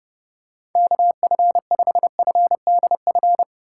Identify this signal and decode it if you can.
Morse code, heard as a keyed tone, about 30 wpm, 700 Hz. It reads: KF5FDF
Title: USA Call Signs - 6 Characters - 4x Character Spacing - 30wpm